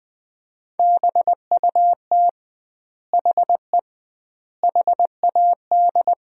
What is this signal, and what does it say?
Morse code, heard as a keyed tone, about 20 wpm, 700 Hz. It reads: BUT HE HAD
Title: 2x Word-Spacing - Sentences from Top 100 Words - 20wpm